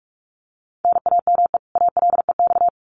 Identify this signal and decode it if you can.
Morse code, heard as a keyed tone, about 44 wpm, 700 Hz. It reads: NAME ALEX
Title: Common QSO Phrases 45wpm